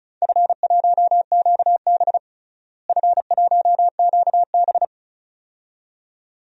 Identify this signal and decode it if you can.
Morse code, heard as a keyed tone, about 36 wpm, 700 Hz. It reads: F1QB F1QB
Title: DXCC-100 Call Signs - 4 Characters - Repeated 2x - 35wpm